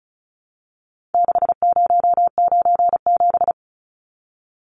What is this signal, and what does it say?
Morse code, heard as a keyed tone, about 34 wpm, 700 Hz. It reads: 6097
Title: Numbers - 4-Digit 35wpm